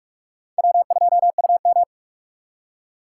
Morse code, heard as a keyed tone, about 56 wpm, 700 Hz. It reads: W2UM
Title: Speed Racing - USA 1x2 Call Signs 45wpm